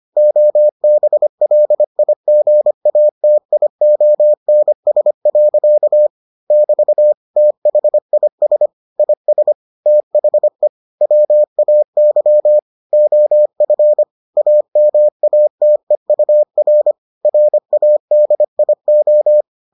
Morse code, heard as a keyed tone, about 25 wpm, 600 Hz. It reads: OBLIGATIONS. <BT> THIS IS THE WAY OF AMATEUR RADIO